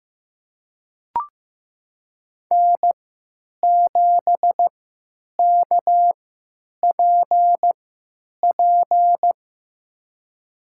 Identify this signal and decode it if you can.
Morse code, heard as a keyed tone, about 15 wpm, spaced as for 9 wpm, 700 Hz. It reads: N7KPP
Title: USA Call Signs - 5 Characters - 3x Character Spacing - 15wpm